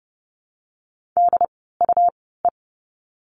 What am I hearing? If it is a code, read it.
Morse code, heard as a keyed tone, about 30 wpm, spaced as for 17 wpm, 700 Hz. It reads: DUE
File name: Lesson 17f - Letter u - Words 30wpm